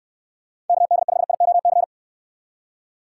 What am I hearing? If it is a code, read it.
Morse code, heard as a keyed tone, about 69 wpm, 700 Hz. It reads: KD5IOC